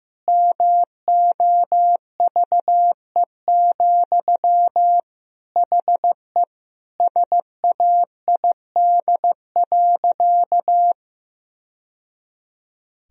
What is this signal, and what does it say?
Morse code, heard as a keyed tone, about 15 wpm, 700 Hz. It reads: MOVE, HE SAID.
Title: Sentences from Top 1000 Words - Review Part 1 - 15wpm